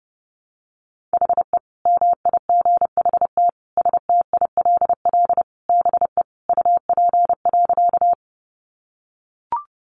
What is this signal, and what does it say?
Morse code, heard as a keyed tone, about 30 wpm, 700 Hz. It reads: HE MIGHT STILL BE UP.
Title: Sentences from Top 300 Words 30wpm